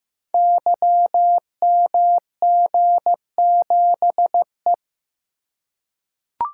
Morse code, heard as a keyed tone, about 15 wpm, 700 Hz. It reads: YMG7E
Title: Code Groups - 5 Letter:Numbers - 15wpm